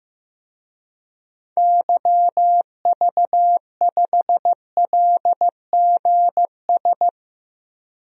Morse code, heard as a keyed tone, about 16 wpm, 700 Hz. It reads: YV5LGS
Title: DXCC-100 Call Signs - 6 Characters - 15wpm